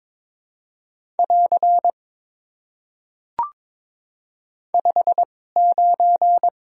Morse code, heard as a keyed tone, about 22 wpm, spaced as for 16 wpm, 700 Hz. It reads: <AR> 59
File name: Top 90 QSO Elements - 2x Character Spacing - 22wpm